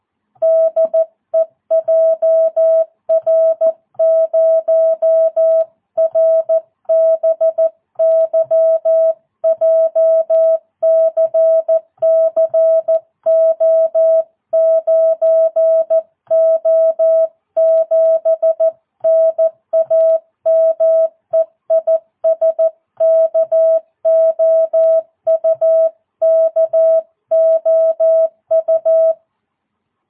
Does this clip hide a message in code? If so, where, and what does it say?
Yes: Morse code from 0.4 to 29.1 s, heard as a keyed tone, about 14 wpm, 650 Hz. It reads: DE JR0RBY JCC O9O7 NAME IS KOU KOU